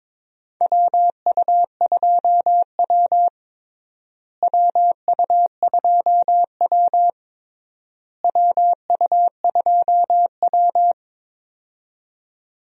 Morse code, heard as a keyed tone, about 22 wpm, 700 Hz. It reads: WU2W WU2W WU2W
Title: USA Call Signs - 4 Characters - Repeated 3x - 22wpm